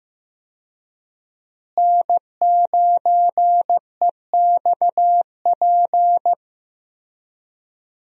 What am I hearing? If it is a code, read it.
Morse code, heard as a keyed tone, about 15 wpm, 700 Hz. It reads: N9EXP
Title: USA Call Signs - 5 Characters - Repeated 3x - 15wpm